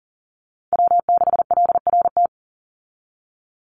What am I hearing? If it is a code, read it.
Morse code, heard as a keyed tone, about 40 wpm, 700 Hz. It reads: W6LRT
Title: USA Call Signs 40wpm